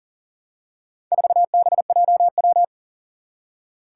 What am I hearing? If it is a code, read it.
Morse code, heard as a keyed tone, about 40 wpm, 700 Hz. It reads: 4BJW
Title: Four Challenging Letter-Number Combinations 40wpm